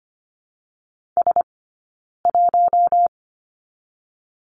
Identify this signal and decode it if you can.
Morse code, heard as a keyed tone, about 25 wpm, 700 Hz. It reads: S 1
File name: course / 25:10wpm / 092 Lesson 16d - Letter c - Two Characters 25wpm